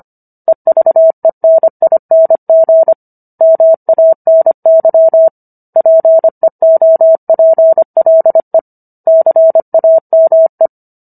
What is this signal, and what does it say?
Morse code, heard as a keyed tone, about 25 wpm, 650 Hz. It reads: EVENING MANY PEOPLE CAME
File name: Book One / 16-Chapter 16 - 25 WPM